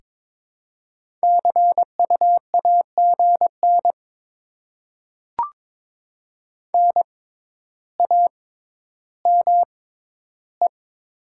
Morse code, heard as a keyed tone, about 21 wpm, spaced as for 11 wpm, 700 Hz. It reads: CU AGN NAME
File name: Top 20 QSO Elements - 6x Character Spacing - 22wpm